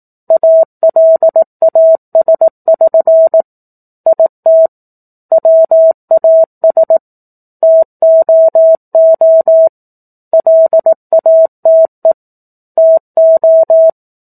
Morse code, heard as a keyed tone, about 18 wpm, 650 Hz. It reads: ALAS<SN> IT WAS TOO LATE TO
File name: 04-Chapter-04 18 WPM